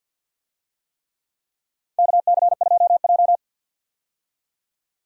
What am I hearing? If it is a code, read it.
Morse code, heard as a keyed tone, about 51 wpm, 700 Hz. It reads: KC2J
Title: USA Call Signs - 4 Characters - 50wpm